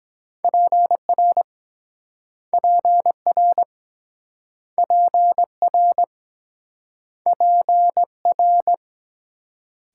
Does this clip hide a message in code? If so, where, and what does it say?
Yes: Morse code from 0.4 to 8.7 s, heard as a keyed tone, about 23 wpm, 700 Hz. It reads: PR PR PR PR